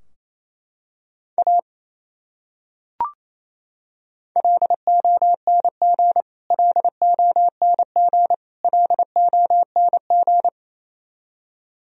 Morse code, heard as a keyed tone, about 28 wpm, 700 Hz. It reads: A LONG LONG LONG
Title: Top 100 Words Repeated 28wpm